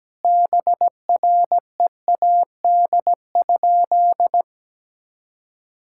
Morse code, heard as a keyed tone, about 17 wpm, 700 Hz. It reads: BREAD?
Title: Sentences from Top 1600 Words 17wpm